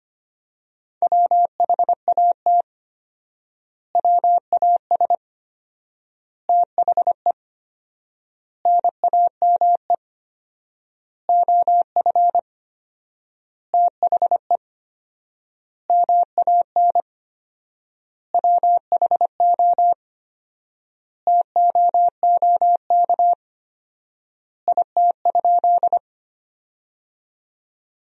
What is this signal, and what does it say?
Morse code, heard as a keyed tone, about 25 wpm, 700 Hz. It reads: WHAT WAS THE NAME OF THE MAN WHO TOOK IT?
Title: 4x Word-Spacing - Sentences from Top 300 Words - 25wpm